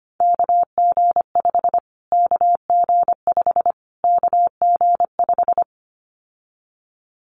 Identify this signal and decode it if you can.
Morse code, heard as a keyed tone, about 25 wpm, 700 Hz. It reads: KG5 KG5 KG5
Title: Callsign Prefixes for Top 25 DXCC - Repeated 3x - 25wpm